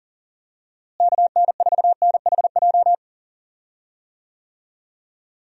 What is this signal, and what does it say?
Morse code, heard as a keyed tone, about 40 wpm, 700 Hz. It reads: KN4NHJ
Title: USA Call Signs - 6 Characters - Repeated 2x - 40wpm